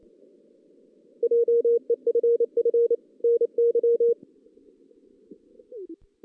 Morse code, heard as a keyed tone, about 27 wpm, 470 Hz. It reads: JEFF NY
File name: 2016 JAN CW NAQP